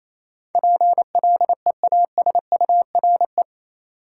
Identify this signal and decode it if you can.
Morse code, heard as a keyed tone, about 28 wpm, 700 Hz. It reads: PLEASURE
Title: Sentences from Top 1600 Words - New Word Drill Alt - 28wpm